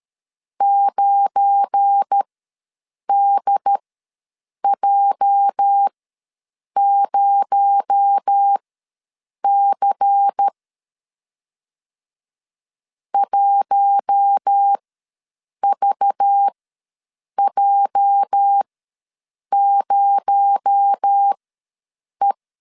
Morse code, heard as a keyed tone, about 13 wpm, spaced as for 7 wpm, 800 Hz. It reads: 9DJ0C 1VJ0E
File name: LEZIONI / 8 YJ45 / 8 YJ ws 12wpm 3342